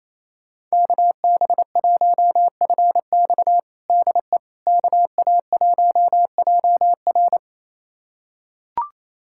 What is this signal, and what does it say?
Morse code, heard as a keyed tone, about 27 wpm, 700 Hz. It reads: KB1FX DE KA1JR
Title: USA Callsign DE Callsign - 5 Characters - 28wpm